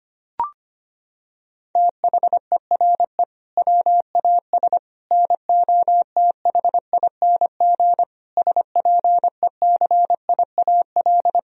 Morse code, heard as a keyed tone, about 25 wpm, 700 Hz. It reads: THERE WAS NOTHING SPECIAL
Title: Sentences from Top 1000 Words - Review Part 3 - 25wpm